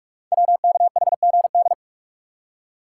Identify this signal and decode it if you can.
Morse code, heard as a keyed tone, about 44 wpm, 700 Hz. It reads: WKHGD
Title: Code Groups - 5 Letters - 45wpm